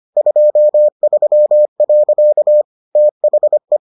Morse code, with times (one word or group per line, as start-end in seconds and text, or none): 0.2-2.6 23.
3.0-3.8 THE